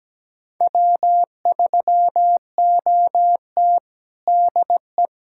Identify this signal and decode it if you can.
Morse code, heard as a keyed tone, about 17 wpm, 700 Hz. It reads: W3OT DE
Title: USA Callsign DE Callsign - 4 Characters - 17wpm